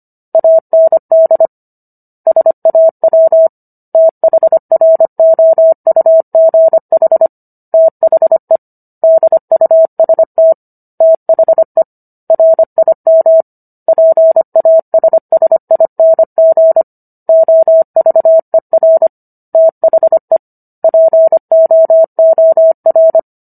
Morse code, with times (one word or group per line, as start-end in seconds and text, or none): 0.3-1.5 AND
2.3-3.5 SAW
3.9-7.3 THROUGH
7.7-8.6 THE
9.0-10.5 DUST
11.0-11.8 THE
12.3-13.4 RIM
13.9-16.8 PASSING
17.3-19.1 OVER
19.5-20.4 THE
20.8-23.2 POOR